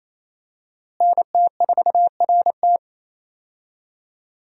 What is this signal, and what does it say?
Morse code, heard as a keyed tone, about 28 wpm, 700 Hz. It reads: NT4RT
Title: USA Call Signs - 5 Characters - Repeated 3x - 28wpm